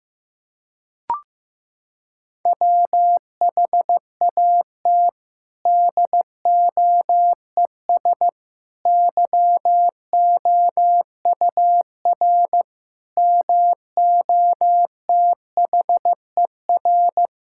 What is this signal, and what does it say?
Morse code, heard as a keyed tone, about 14 wpm, 700 Hz. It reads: WHAT DOES YOUR MOTHER
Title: Sentences from Top 1000 Words - Review Part 5 - 15wpm